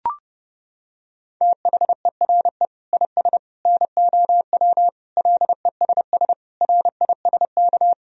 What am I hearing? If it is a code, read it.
Morse code, heard as a keyed tone, about 32 wpm, 700 Hz. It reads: THERE IS NOW LESS RISK
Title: Sentences from Top 1000 Words - Review Part 4 - 30wpm